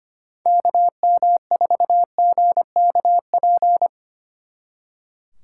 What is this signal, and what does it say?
Morse code, heard as a keyed tone, about 32 wpm, 700 Hz. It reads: KM4GKP